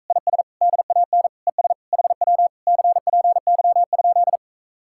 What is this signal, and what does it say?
Morse code, heard as a keyed tone, about 44 wpm, 700 Hz. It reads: IS DAN ES HW CPY?